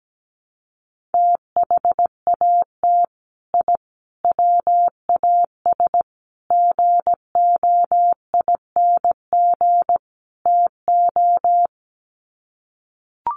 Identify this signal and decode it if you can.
Morse code, heard as a keyed tone, about 17 wpm, 700 Hz. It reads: THAT I WAS GOING TO
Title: Sets of 5 Words Encore 17wpm